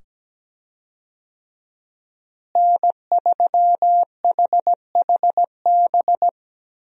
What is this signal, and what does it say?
Morse code, heard as a keyed tone, about 17 wpm, 700 Hz. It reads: N3HHB
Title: USA Call Signs - 5 Characters - Repeated 2x - 17wpm